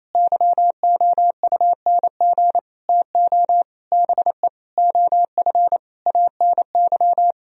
Morse code, heard as a keyed tone, about 28 wpm, 700 Hz. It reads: YOUNG TO BE OF ANY
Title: Sentences from Top 1000 Words - Review Part 4 - 28wpm